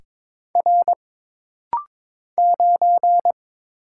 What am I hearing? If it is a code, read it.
Morse code, heard as a keyed tone, about 22 wpm, 700 Hz. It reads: R 9
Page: This rapid-fire timing is geared to push your ICR (Instant Character Recognition) skills.